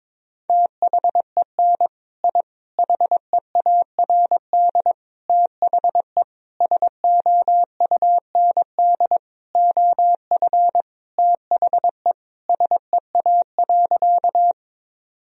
Morse code, heard as a keyed tone, about 22 wpm, 700 Hz. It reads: THEN I HEARD THE SOUND OF THE SEA.
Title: Sentences from Top 1000 Words - Review Part 1 - 22wpm